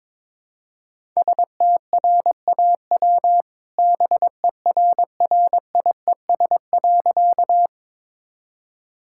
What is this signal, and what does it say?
Morse code, heard as a keyed tone, about 22 wpm, 700 Hz. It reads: STRAW BERRIES.
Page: morse-code-ninja-practice-22wpm